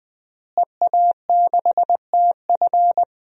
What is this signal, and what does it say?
Morse code, heard as a keyed tone, about 20 wpm, 700 Hz. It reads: EA6TF
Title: DXCC-100 Call Signs - 5 Characters - Repeated 3x - 20wpm